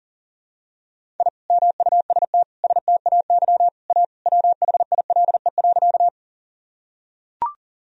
Morse code, heard as a keyed tone, about 40 wpm, 700 Hz. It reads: I MUST STAY A WHILE.
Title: Sentences from Top 1000 Words - Review Part 1 - 40wpm